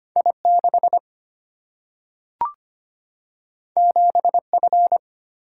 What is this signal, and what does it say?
Morse code, heard as a keyed tone, about 25 wpm, 700 Hz. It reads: I 6 7 F
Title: Single-Letter Single-Number 25wpm